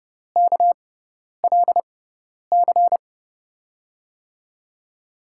Morse code, heard as a keyed tone, about 30 wpm, spaced as for 10 wpm, 700 Hz. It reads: KLC
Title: USA Call Signs - 6 Characters - 6x Character Spacing - 30wpm